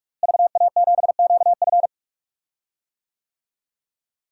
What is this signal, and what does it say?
Morse code, heard as a keyed tone, about 46 wpm, 700 Hz. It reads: VA7QF
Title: DXCC-100 Call Signs - 5 Characters - Repeated 3x - 45wpm